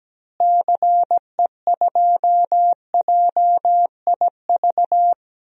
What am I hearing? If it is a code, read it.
Morse code, heard as a keyed tone, about 17 wpm, 700 Hz. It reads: CE2JIV
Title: DXCC-100 Call Signs - 6 Characters - Repeated 3x - 17wpm